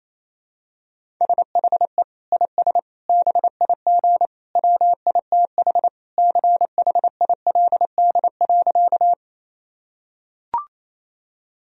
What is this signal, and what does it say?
Morse code, heard as a keyed tone, about 28 wpm, 700 Hz. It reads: SHE IS BIG WITH CHILD.
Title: Sentences from Top 1000 Words - Review Part 1 - 28wpm